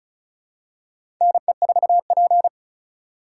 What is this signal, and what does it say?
Morse code, heard as a keyed tone, about 36 wpm, 700 Hz. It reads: NE4P
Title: USA Call Signs - 4x Character Spacing - 35wpm